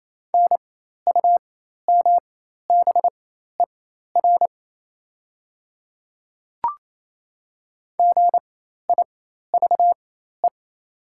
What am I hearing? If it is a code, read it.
Morse code, heard as a keyed tone, about 27 wpm, spaced as for 13 wpm, 690 Hz. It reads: NUMBER GIVE
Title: 4x Character Spacing - Top 200 Words - 28wpm